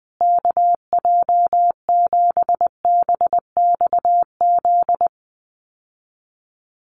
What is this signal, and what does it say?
Morse code, heard as a keyed tone, about 20 wpm, 700 Hz. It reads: KJ7BXZ
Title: USA Call Signs Repeated 3x 20wpm